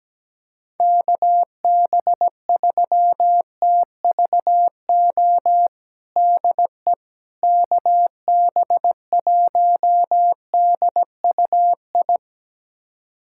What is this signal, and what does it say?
Morse code, heard as a keyed tone, about 17 wpm, 700 Hz. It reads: KB3TVO DE KB1DUI
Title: USA Callsign DE Callsign - 6 Characters - 17wpm